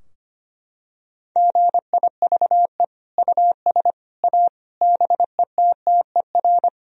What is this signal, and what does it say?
Morse code, heard as a keyed tone, about 25 wpm, 700 Hz. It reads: GIVE US A BETTER
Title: Sentences from Top 1000 Words - Review Part 4 - 25wpm